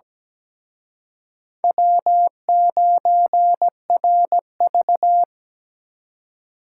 Morse code, heard as a keyed tone, about 20 wpm, 700 Hz. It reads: W9RV
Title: Speed Racing - USA 1x2 Call Signs 17wpm